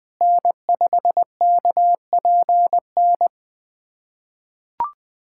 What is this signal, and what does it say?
Morse code, heard as a keyed tone, about 20 wpm, 700 Hz. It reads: N5KPN
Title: USA Callsign DE Callsign - 5 Characters - 20wpm